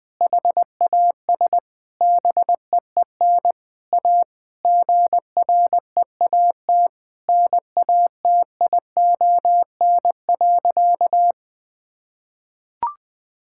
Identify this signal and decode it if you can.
Morse code, heard as a keyed tone, about 20 wpm, 700 Hz. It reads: HAS BEEN A GREAT NATION.
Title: Sentences from Top 1000 Words - Review Part 6 - 20wpm